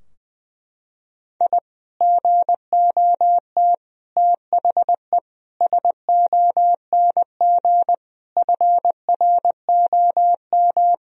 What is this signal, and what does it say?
Morse code, heard as a keyed tone, about 20 wpm, 700 Hz. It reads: I GOT THE SONG FROM
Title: Sentences from Top 1000 Words - Review Part 2 - 20wpm